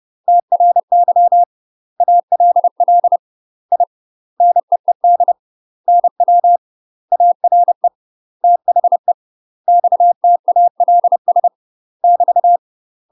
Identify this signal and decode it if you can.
Morse code, heard as a keyed tone, about 30 wpm, 700 Hz. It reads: TRY ALL I NEED NW ARE THE XTALS <BT>
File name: UK MASTER QSOs TRAINING / 30WPM / UK MASTER QSOs 0033